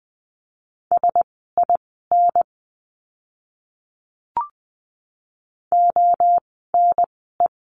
Morse code, heard as a keyed tone, about 20 wpm, spaced as for 14 wpm, 700 Hz. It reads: SIN ONE
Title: 054 Lesson 10f - Letter r - Words 20wpm